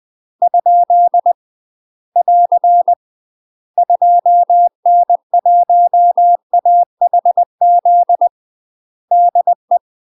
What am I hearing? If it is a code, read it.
Morse code, heard as a keyed tone, about 20 wpm, 700 Hz. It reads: ? <AR> 2N1AHZ DE
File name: UK MASTER QSOs TRAINING / 20WPM / UK MASTER QSOs 0046